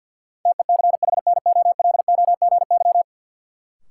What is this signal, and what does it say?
Morse code, heard as a keyed tone, about 49 wpm, 700 Hz. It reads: TECHNOLOGY
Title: Sentences from Top 1200 Words - New Word Drill Alt - 50wpm